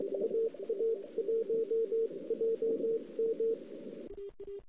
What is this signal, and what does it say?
Morse code, heard as a keyed tone, about 22 wpm, 450 Hz. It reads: AA1JM
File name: CW / 2008-ss-cw-k5tr-op-n5rz -02:00